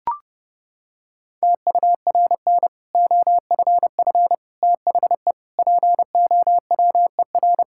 Morse code, heard as a keyed tone, about 32 wpm, 700 Hz. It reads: TURN OFF THE POWER
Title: Sentences from Top 1000 Words - Review Part 6 - 30wpm